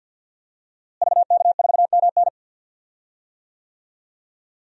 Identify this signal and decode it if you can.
Morse code, heard as a keyed tone, about 52 wpm, 700 Hz. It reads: VK4MN
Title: DXCC-100 Call Signs - 5 Characters - Repeated 2x - 50wpm